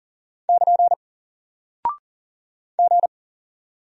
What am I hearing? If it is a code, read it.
Morse code, heard as a keyed tone, about 40 wpm, 700 Hz. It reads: <KN> G
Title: Single Character (Letter:Number:Punctuation:Prosign) Alt 40wpm